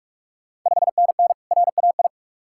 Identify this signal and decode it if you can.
Morse code, heard as a keyed tone, about 44 wpm, 690 Hz. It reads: 5NN RAI
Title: Ontario QSO Party Exchange 45wpm